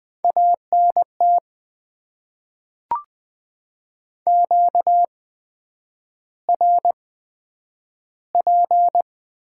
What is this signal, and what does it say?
Morse code, heard as a keyed tone, about 20 wpm, spaced as for 9 wpm, 700 Hz. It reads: ANT QRP
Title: Top 90 QSO Elements - 8x Character Spacing - 20wpm